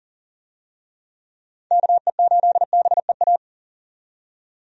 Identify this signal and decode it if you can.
Morse code, heard as a keyed tone, about 40 wpm, 700 Hz. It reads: KE8BEA
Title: USA Call Signs - 6 Characters - 40wpm